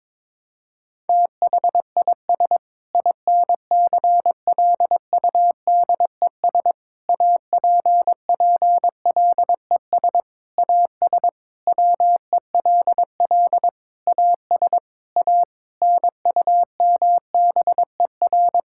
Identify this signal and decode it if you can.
Morse code, heard as a keyed tone, about 22 wpm, 700 Hz. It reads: THIS INCLUDES APPLES AS WELL AS A NUMBER
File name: practice / 22wpm / Sentences with 7-Word Phrases - 22wpm